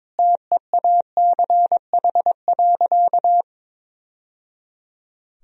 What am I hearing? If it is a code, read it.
Morse code, heard as a keyed tone, about 22 wpm, 700 Hz. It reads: TEACH.
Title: Sentences from Top 1000 Words - Review Part 2 - 22wpm